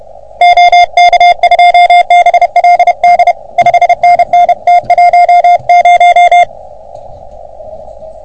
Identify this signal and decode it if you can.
Morse code, heard as a keyed tone, about 33 wpm, 720 Hz. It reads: OK2BLD 5NN T10
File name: OM3PA PILUP CUC 388